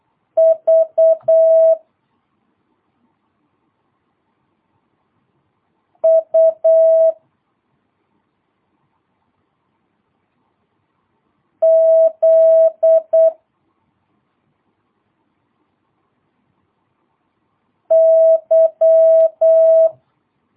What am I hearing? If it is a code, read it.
Morse code, heard as a keyed tone, about 8 wpm, 650 Hz. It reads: V U Z Y
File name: jh8cbh / 2022cw